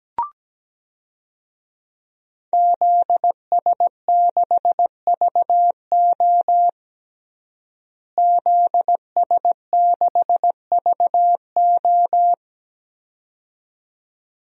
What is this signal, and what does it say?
Morse code, heard as a keyed tone, about 17 wpm, 700 Hz. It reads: ZS6VO ZS6VO
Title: DXCC-100 Call Signs - 5 Characters - Repeated 2x - 17wpm